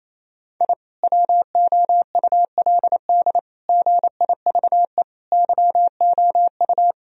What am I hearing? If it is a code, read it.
Morse code, heard as a keyed tone, about 28 wpm, 700 Hz. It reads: I WOULD GIVE YOU
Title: Sentences from Top 1000 Words - Review Part 1 - 28wpm